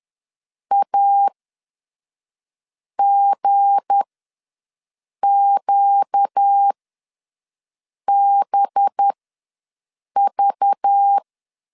Morse code, heard as a keyed tone, about 10 wpm, spaced as for 5 wpm, 800 Hz. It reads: AGQBV
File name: LEZIONI / 7 GF9Z / 7 G ws 10wpm 3843